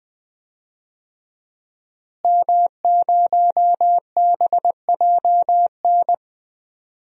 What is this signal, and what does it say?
Morse code, heard as a keyed tone, about 20 wpm, 700 Hz. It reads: M0BJN
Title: DXCC-100 Call Signs - 5 Characters - Repeated 3x - 20wpm